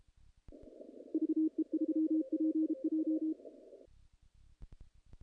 Morse code, heard as a keyed tone, about 32 wpm, 320 Hz. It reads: VE3PJ
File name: contests / naqp-cw-k5tr-op-k5pi- -1800